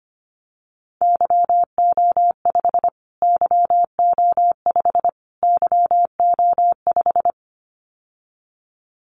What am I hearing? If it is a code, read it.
Morse code, heard as a keyed tone, about 25 wpm, 700 Hz. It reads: YO5 YO5 YO5
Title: Callsign Prefixes for Top 25 DXCC - Repeated 3x - 25wpm